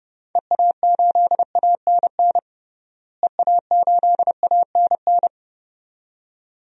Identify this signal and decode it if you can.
Morse code, heard as a keyed tone, about 30 wpm, 700 Hz. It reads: EA8ANN EA8ANN
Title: DXCC-100 Call Signs - 6 Characters - Repeated 2x - 30wpm